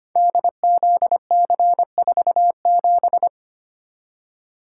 Morse code, heard as a keyed tone, about 25 wpm, 700 Hz. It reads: DZC47
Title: Code Groups - 5 Letter:Numbers - 25wpm